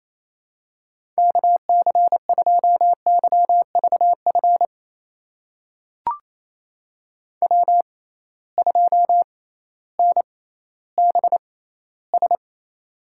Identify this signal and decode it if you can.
Morse code, heard as a keyed tone, about 27 wpm, spaced as for 15 wpm, 700 Hz. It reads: KC2YVF W2NBS
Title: USA Call Signs - 6x Character Spacing - 28wpm